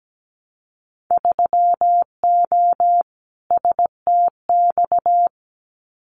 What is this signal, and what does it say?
Morse code, heard as a keyed tone, about 17 wpm, 700 Hz. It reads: 3O STX
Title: Winter Field Day Exchanges 17wpm